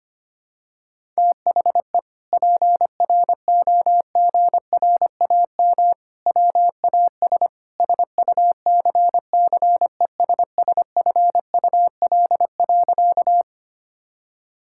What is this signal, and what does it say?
Morse code, heard as a keyed tone, about 25 wpm, 700 Hz. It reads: THE PROGRAM WAS SUCCESSFUL.
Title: Sentences from Top 1000 Words - Review Part 5 - 25wpm